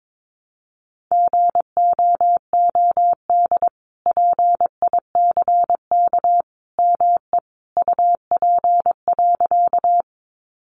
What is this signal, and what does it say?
Morse code, heard as a keyed tone, about 22 wpm, 700 Hz. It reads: GOOD PICK ME UP.
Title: Sentences from Top 1000 Words 22wpm